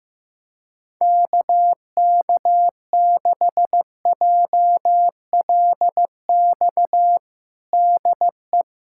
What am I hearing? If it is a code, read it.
Morse code, heard as a keyed tone, about 15 wpm, 700 Hz. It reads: KK6JLX DE
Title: USA Callsign DE Callsign 15wpm